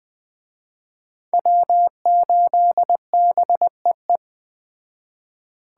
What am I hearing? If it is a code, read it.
Morse code, heard as a keyed tone, about 20 wpm, 700 Hz. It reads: W8BEE
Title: Speed Racing - USA 1x3 Call Signs 17wpm